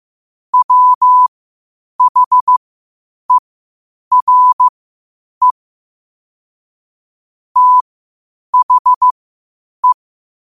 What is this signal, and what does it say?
Morse code, heard as a keyed tone, about 15 wpm, spaced as for 8 wpm, 1000 Hz. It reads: WHERE THE
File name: qotdfiles / 8wpm